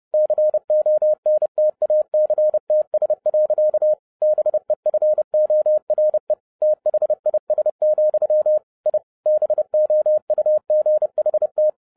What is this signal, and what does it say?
Morse code, heard as a keyed tone, about 30 wpm, 610 Hz. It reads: CONTACTS. BEFORE THIS, I BOUGHT